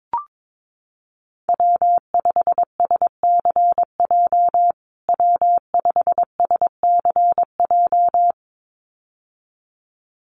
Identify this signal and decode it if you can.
Morse code, heard as a keyed tone, about 22 wpm, 700 Hz. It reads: W5SCJ W5SCJ
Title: USA Call Signs Repeated 2x 22wpm